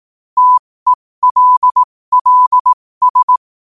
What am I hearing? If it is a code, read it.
Morse code, heard as a keyed tone, about 18 wpm, spaced as for 15 wpm, 1000 Hz. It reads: TELLS
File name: qotdfiles / 15wpm